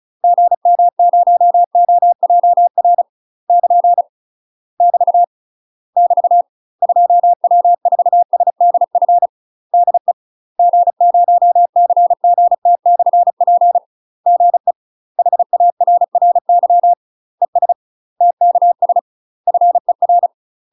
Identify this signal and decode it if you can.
Morse code, heard as a keyed tone, about 34 wpm, 700 Hz. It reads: GM0OJR <KN> <BT> <BT> 2W4SDF DE G0CGT/P GE HARRY ES TKS FER
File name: UK MASTER QSOs TRAINING/35WPM